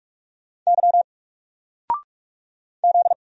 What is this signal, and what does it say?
Morse code, heard as a keyed tone, about 44 wpm, 700 Hz. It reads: Y Z
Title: Single Character (Letter:Number:Punctuation:Prosign) 45wpm